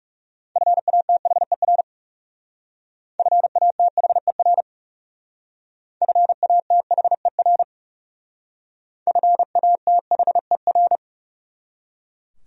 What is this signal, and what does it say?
Morse code, heard as a keyed tone, about 38 wpm, 700 Hz. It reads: FATHER FATHER FATHER FATHER